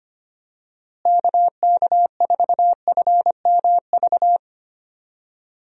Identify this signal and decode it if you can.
Morse code, heard as a keyed tone, about 25 wpm, 700 Hz. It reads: KK4FMV
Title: USA Call Signs - 6 Characters - Repeated 2x - 25wpm